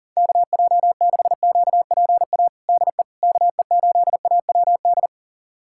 Morse code, heard as a keyed tone, about 40 wpm, 700 Hz. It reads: KJ6QPA DE KE8AWD
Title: USA Callsign DE Callsign - 6 Characters - 40wpm